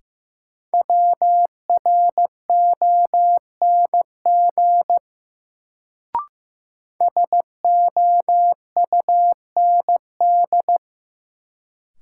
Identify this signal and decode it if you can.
Morse code, heard as a keyed tone, about 15 wpm, 700 Hz. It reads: WRONG SOUND
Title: Sentences from Top 900 Words - New Word Drill - 15wpm